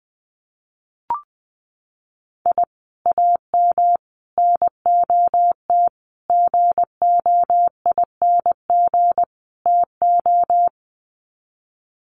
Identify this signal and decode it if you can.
Morse code, heard as a keyed tone, about 20 wpm, 700 Hz. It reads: I AM NOT GOING TO
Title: Sets of 5 Words Encore 20wpm